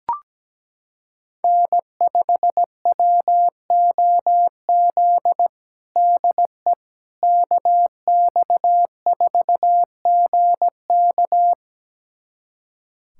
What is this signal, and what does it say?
Morse code, heard as a keyed tone, about 17 wpm, 700 Hz. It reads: N5WOZ DE KX4GK
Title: USA Callsign DE Callsign - 5 Characters - 17wpm